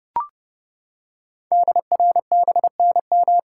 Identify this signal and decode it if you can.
Morse code, heard as a keyed tone, about 30 wpm, 700 Hz. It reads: DRBNM
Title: Code Groups - 5 Letters - 30wpm